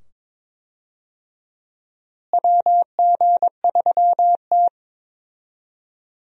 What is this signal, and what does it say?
Morse code, heard as a keyed tone, about 22 wpm, 700 Hz. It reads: WG3T
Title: USA Call Signs - 4 Characters - Repeated 2x - 22wpm